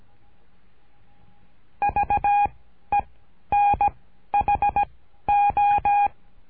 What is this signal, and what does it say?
Morse code, heard as a keyed tone, about 17 wpm, 820 Hz. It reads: V E N H O
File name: cursos / CURSO DE TELEGRAFIA / Telegrafia - Trilha 13